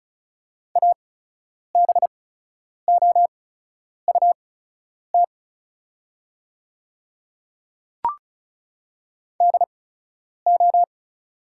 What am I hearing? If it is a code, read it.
Morse code, heard as a keyed tone, about 36 wpm, spaced as for 10 wpm, 700 Hz. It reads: ABOUT DO
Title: 8x Character Spacing - Top 100 Words - 35wpm